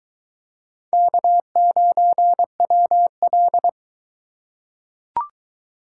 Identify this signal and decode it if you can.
Morse code, heard as a keyed tone, about 20 wpm, 700 Hz. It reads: K9WL